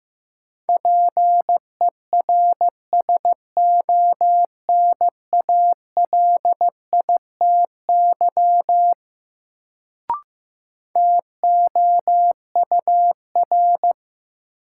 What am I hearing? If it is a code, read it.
Morse code, heard as a keyed tone, about 15 wpm, 700 Hz. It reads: PERSONALITY TOUR
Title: Sentences from Top 1400 Words - New Word Drill - 15wpm